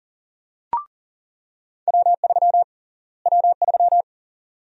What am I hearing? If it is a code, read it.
Morse code, heard as a keyed tone, about 40 wpm, 700 Hz. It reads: W3 W3
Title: USA Call Sign Prefixes - Repeated 2x - 40wpm